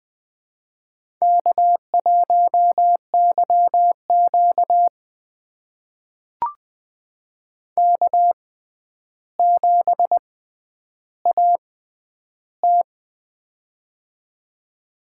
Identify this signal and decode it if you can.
Morse code, heard as a keyed tone, about 20 wpm, spaced as for 10 wpm, 700 Hz. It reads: K1YQ K7AT
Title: USA Call Signs - 4 Characters - 6x Character Spacing - 20wpm